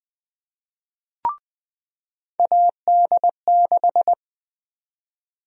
Morse code, heard as a keyed tone, about 20 wpm, 700 Hz. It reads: AD6
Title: USA Call Sign Prefixes - 20wpm